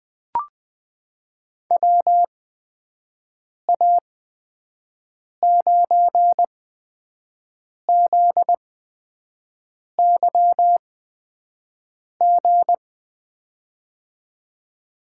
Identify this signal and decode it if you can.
Morse code, heard as a keyed tone, about 20 wpm, spaced as for 5 wpm, 700 Hz. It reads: WA9ZYG
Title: USA Call Signs - 8x Character Spacing - 20wpm